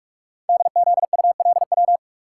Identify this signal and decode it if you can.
Morse code, heard as a keyed tone, about 44 wpm, 700 Hz. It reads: DZULW
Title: Code Groups - 5 Letters - 45wpm